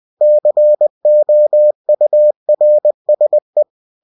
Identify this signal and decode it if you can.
Morse code, heard as a keyed tone, about 20 wpm, 600 Hz. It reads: COURSE
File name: cw-books / winnie-the-pooh / winnie-20wpm-page-0077